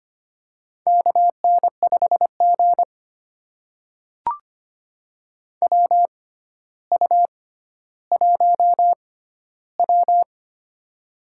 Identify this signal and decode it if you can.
Morse code, heard as a keyed tone, about 25 wpm, spaced as for 13 wpm, 700 Hz. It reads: KN5G WU1W
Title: USA Call Signs - 4 Characters - 6x Character Spacing - 25wpm